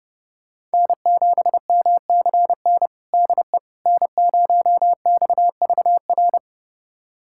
Morse code, heard as a keyed tone, about 30 wpm, 700 Hz. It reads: N7MCN DE N0XVR
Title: USA Callsign DE Callsign - 5 Characters - 30wpm